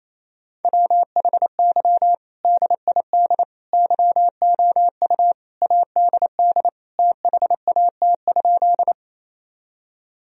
Morse code, heard as a keyed tone, about 28 wpm, 700 Hz. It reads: WHY DID YOU ADD THAT?
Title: Sentences from Top 1000 Words - Review Part 1 - 28wpm